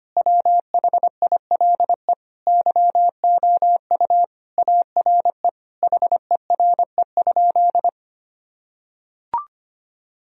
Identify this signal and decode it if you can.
Morse code, heard as a keyed tone, about 25 wpm, 700 Hz. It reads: WHILE YOU ARE HERE?
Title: Sentences from Top 1000 Words - Review Part 1 - 25wpm